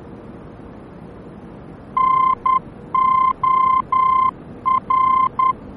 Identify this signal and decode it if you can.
Morse code, heard as a keyed tone, about 10 wpm, 1040 Hz. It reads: NOR